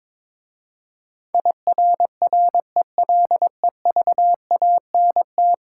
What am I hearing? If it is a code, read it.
Morse code, heard as a keyed tone, about 22 wpm, 700 Hz. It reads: IRRELEVANT
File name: practice / 22wpm / 10 Letter Words - First 3 Letters Given - 22wpm